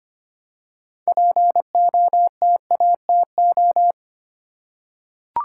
Morse code, heard as a keyed tone, about 25 wpm, 700 Hz. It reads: POTATO
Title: Six Letter Words - 25wpm